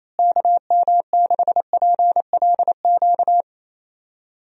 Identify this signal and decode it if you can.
Morse code, heard as a keyed tone, about 28 wpm, 700 Hz. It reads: KM6PLQ
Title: USA Call Signs - 6 Characters - Repeated 3x - 28wpm